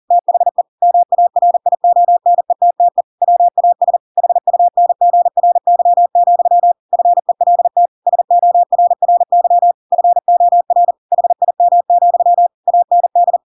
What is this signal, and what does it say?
Morse code, heard as a keyed tone, about 40 wpm, 700 Hz. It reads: THE MARIONETTE WAS HUNGRY, FELT SORRY FOR HIM, AND
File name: Pinocchio / Pinocchio ch7 40 40 ik1hge